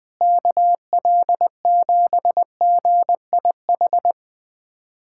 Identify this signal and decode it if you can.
Morse code, heard as a keyed tone, about 20 wpm, 700 Hz. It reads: KL7GIH
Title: USA Callsign DE Callsign - 6 Characters - 20wpm